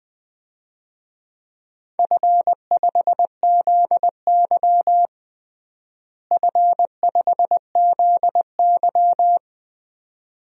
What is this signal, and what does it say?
Morse code, heard as a keyed tone, about 21 wpm, 700 Hz. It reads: F5ZY F5ZY
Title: DXCC-100 Call Signs - 4 Characters - Repeated 2x - 20wpm